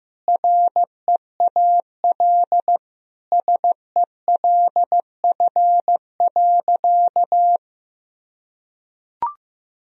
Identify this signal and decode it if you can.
Morse code, heard as a keyed tone, about 15 wpm, 700 Hz. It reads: REAL SELF.
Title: Sentences from Top 1000 Words - Review Part 1 - 15wpm